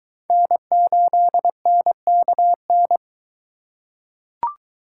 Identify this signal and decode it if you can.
Morse code, heard as a keyed tone, about 22 wpm, 700 Hz. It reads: N8NKN